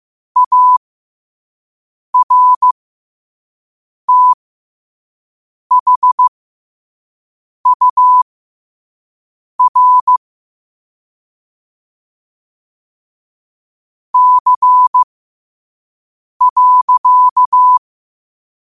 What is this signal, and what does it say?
Morse code, heard as a keyed tone, about 15 wpm, spaced as for 5 wpm, 1000 Hz. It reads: ARTHUR C.
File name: qotdfiles / 5wpm